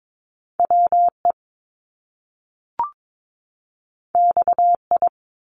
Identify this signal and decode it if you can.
Morse code, heard as a keyed tone, about 22 wpm, 700 Hz. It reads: WE XI
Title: Top Two Letter Combinations in Words 22wpm